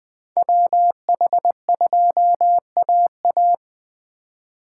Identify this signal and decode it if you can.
Morse code, heard as a keyed tone, about 21 wpm, 700 Hz. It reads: WH2AA
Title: USA Call Signs - 5 Characters - 2x Character Spacing - 20wpm